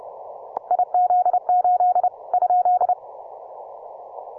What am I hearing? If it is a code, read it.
Morse code, heard as a keyed tone, about 30 wpm, 720 Hz. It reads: IZ8 ?